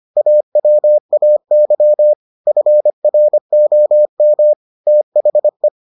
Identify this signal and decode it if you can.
Morse code, heard as a keyed tone, about 25 wpm, 600 Hz. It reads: AWAY FROM THE